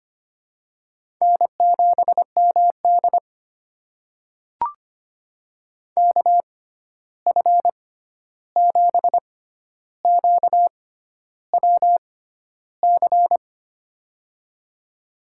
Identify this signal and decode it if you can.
Morse code, heard as a keyed tone, about 25 wpm, spaced as for 11 wpm, 700 Hz. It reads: N7MD KF7QWC
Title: USA Call Signs - 6x Character Spacing - 25wpm